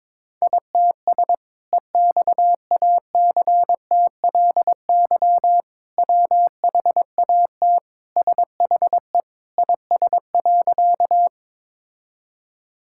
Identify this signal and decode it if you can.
Morse code, heard as a keyed tone, about 22 wpm, 700 Hz. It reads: ITS EXACTLY WHAT SHE IS.
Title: Sentences from Top 1000 Words - Review Part 6 - 22wpm